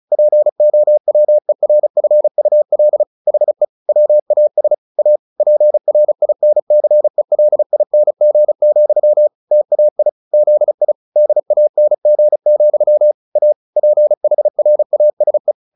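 Morse code, heard as a keyed tone, about 34 wpm, 600 Hz. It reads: POWERFUL HE WAS A PRINCELING, TAI ZI DANG, A PHRASE
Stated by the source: Morse practice at 35 wpm from CW med Gnister, sending Ken Follett's novel Never in English.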